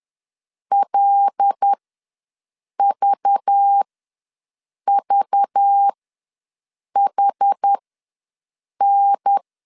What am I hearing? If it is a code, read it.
Morse code, heard as a keyed tone, about 10 wpm, spaced as for 6 wpm, 800 Hz. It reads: LVVHN
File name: LEZIONI / 5 LBVC / 5 LBV ws 10wpm 3645